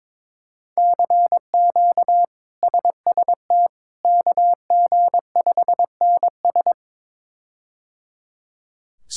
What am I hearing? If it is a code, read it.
Morse code, heard as a keyed tone, about 21 wpm, 700 Hz. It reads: CQ SST KG5NS
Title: K1USN SST QSO Practice - 22wpm